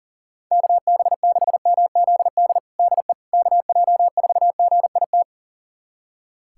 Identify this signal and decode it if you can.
Morse code, heard as a keyed tone, about 37 wpm, 700 Hz. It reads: KB6MZD DE KJ4GIT
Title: USA Callsign DE Callsign - 6 Characters - 40wpm